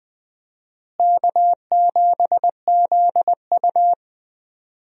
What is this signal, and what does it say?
Morse code, heard as a keyed tone, about 20 wpm, 700 Hz. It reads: K7ZU
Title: USA Call Signs - 4 Characters - Repeated 2x - 20wpm